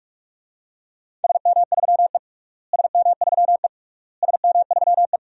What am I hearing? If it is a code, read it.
Morse code, heard as a keyed tone, about 46 wpm, 700 Hz. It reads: SM3E SM3E SM3E
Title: DXCC-100 Call Signs - 4 Characters - Repeated 3x - 45wpm